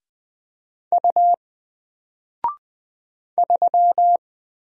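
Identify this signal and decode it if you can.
Morse code, heard as a keyed tone, about 20 wpm, 700 Hz. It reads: U 3
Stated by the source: Single letter and number Morse drill at 20 wpm, from N4LCD.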